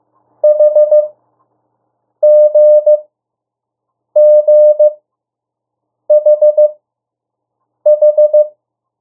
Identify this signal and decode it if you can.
Morse code, heard as a keyed tone, about 15 wpm, spaced as for 6 wpm, 600 Hz. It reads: HGGHH
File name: morse mp3 / 02 urok 1